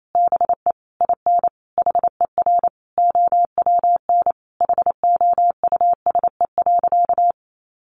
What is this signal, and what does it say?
Morse code, heard as a keyed tone, about 28 wpm, 700 Hz. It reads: BE IN HER OWN HOUSE.
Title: Sentences from Top 300 Words 28wpm